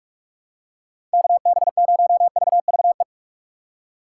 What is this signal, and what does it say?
Morse code, heard as a keyed tone, about 43 wpm, 700 Hz. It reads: KB0VVE
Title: USA Call Signs - 6 Characters - 4x Character Spacing - 45wpm